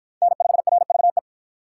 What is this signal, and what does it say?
Morse code, heard as a keyed tone, about 53 wpm, 700 Hz. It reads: N5RVE